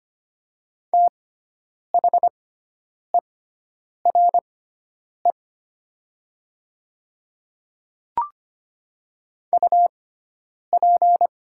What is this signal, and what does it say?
Morse code, heard as a keyed tone, about 25 wpm, spaced as for 9 wpm, 700 Hz. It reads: THERE UP
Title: 6x Character Spacing - Top 100 Words - 25wpm